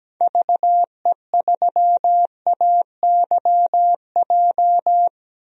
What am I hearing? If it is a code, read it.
Morse code, heard as a keyed tone, about 17 wpm, 700 Hz. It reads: VE3AYJ
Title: DXCC-100 Call Signs - 6 Characters - 17wpm